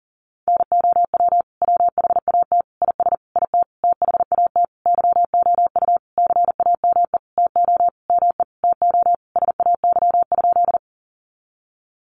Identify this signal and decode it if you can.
Morse code, heard as a keyed tone, about 40 wpm, 700 Hz. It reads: NOW WHAT IS IT THAT YOU CAME TO ME TO SAY?
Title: Sentences from Top 200 Words 40wpm